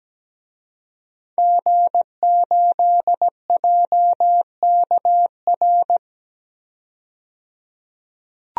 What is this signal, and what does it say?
Morse code, heard as a keyed tone, about 17 wpm, 700 Hz. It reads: G8JKR
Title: DXCC-100 Call Signs - 5 Characters - 17wpm